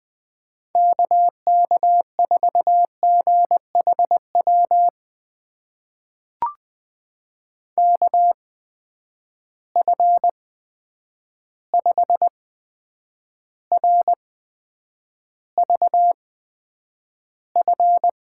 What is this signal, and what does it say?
Morse code, heard as a keyed tone, about 20 wpm, spaced as for 9 wpm, 700 Hz. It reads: KK4GHW KF5RVF
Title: USA Call Signs - 6 Characters - 8x Character Spacing - 20wpm